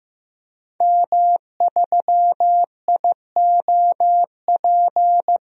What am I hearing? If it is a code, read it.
Morse code, heard as a keyed tone, about 15 wpm, 700 Hz. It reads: M3IOP
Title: DXCC-100 Call Signs - 5 Characters - 15wpm